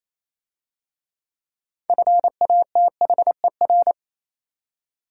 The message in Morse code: FATHER

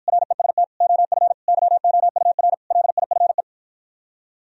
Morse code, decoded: REST OF YOUR LIFE

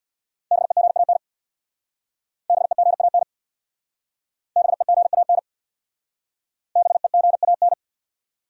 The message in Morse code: BEGAN BEGAN BEGAN BEGAN

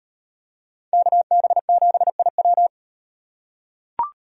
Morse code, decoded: KB7IW

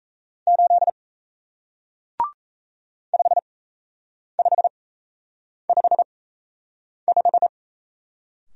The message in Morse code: 8 5555

